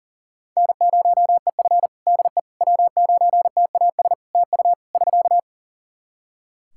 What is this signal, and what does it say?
Morse code, heard as a keyed tone, about 37 wpm, 700 Hz. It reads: N0EF DE W9TAS TU <SK>
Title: Practice QSO - Exchange 4 - Ending - 40wpm